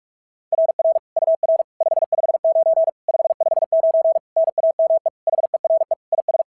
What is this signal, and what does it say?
Morse code, heard as a keyed tone, about 46 wpm, 690 Hz. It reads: RR UR 559 559 NAME HERE IS
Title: SOTA QSO Practice 45wpm